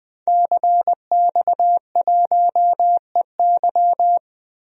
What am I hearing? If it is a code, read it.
Morse code, heard as a keyed tone, about 21 wpm, 700 Hz. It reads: CX1EY
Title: DXCC-100 Call Signs - 5 Characters - Repeated 3x - 20wpm